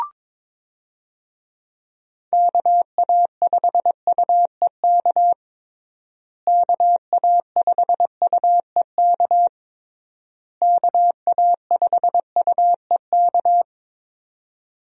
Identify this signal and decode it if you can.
Morse code, heard as a keyed tone, about 22 wpm, 700 Hz. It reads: KA5UEK KA5UEK KA5UEK
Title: USA Call Signs - 6 Characters - Repeated 3x - 22wpm